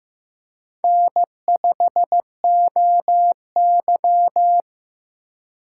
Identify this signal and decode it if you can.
Morse code, heard as a keyed tone, about 15 wpm, 700 Hz. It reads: N5OY
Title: USA Call Signs - 3x Character Spacing - 15wpm